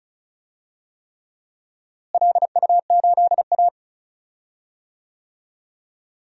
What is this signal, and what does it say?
Morse code, heard as a keyed tone, about 34 wpm, 700 Hz. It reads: LU8A
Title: DXCC-100 Call Signs - 4 Characters - Repeated 3x - 35wpm